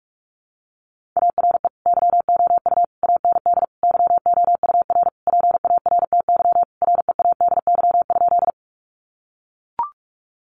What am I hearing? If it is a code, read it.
Morse code, heard as a keyed tone, about 44 wpm, 700 Hz. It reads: ARE YOU AND YOUR PARTY READY?
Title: Sentences from Top 1400 Words 45wpm